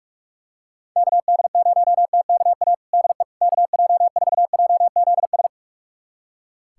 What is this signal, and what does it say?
Morse code, heard as a keyed tone, about 46 wpm, 700 Hz. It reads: KD0TKA DE KJ4JZS
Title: USA Callsign DE Callsign - 6 Characters - 45wpm